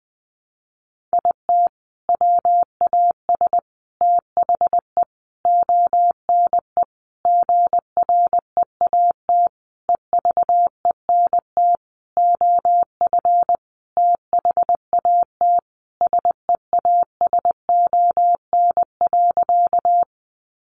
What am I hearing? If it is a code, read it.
Morse code, heard as a keyed tone, about 20 wpm, 700 Hz. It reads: IT WAS THE ONE GREAT EVENT OF THAT SEASON.
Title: Sentences from Top 1400 Words 20wpm